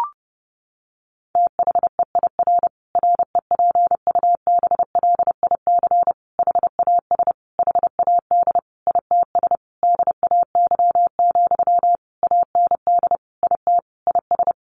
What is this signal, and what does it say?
Morse code, heard as a keyed tone, about 30 wpm, 700 Hz. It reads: THEIR REPUBLIC HAS HAD ITS DAY, AND IT IS
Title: Sentences from Top 1500 Words 30wpm